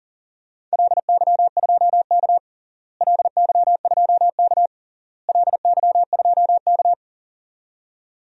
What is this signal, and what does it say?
Morse code, heard as a keyed tone, about 40 wpm, 700 Hz. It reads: LY2K LY2K LY2K
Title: DXCC-100 Call Signs - 4 Characters - Repeated 3x - 40wpm